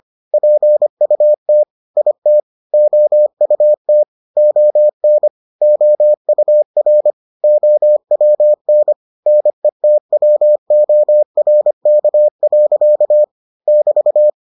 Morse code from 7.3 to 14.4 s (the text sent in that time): OWN NETWORK. <BT>